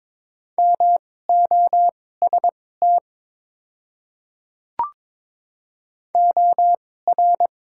Morse code, heard as a keyed tone, about 22 wpm, spaced as for 16 wpm, 700 Hz. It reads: MOST OR
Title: 2x Character Spacing - Top 100 Words - 22wpm